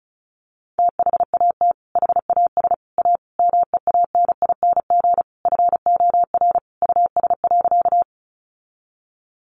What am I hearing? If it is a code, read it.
Morse code, heard as a keyed tone, about 34 wpm, 700 Hz. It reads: THAT HAS A MEANING FOR US.
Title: Sentences from Top 900 Words 35wpm